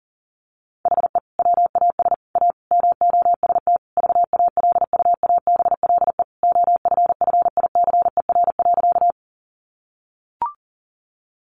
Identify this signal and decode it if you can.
Morse code, heard as a keyed tone, about 40 wpm, 700 Hz. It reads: HE WAS A MOST VALUABLE OFFICER.
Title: Sentences from Top 1200 Words 40wpm